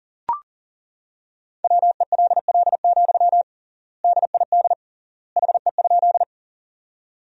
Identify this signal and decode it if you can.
Morse code, heard as a keyed tone, about 40 wpm, 700 Hz. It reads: WELL, DID HE?
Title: 3x Word-Spacing - Sentences from Top 200 Words - 40wpm